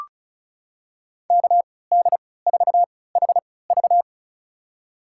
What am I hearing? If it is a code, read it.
Morse code, heard as a keyed tone, about 36 wpm, spaced as for 20 wpm, 700 Hz. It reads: KD4HV
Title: USA Call Signs - 5 Characters - 3x Character Spacing - 35wpm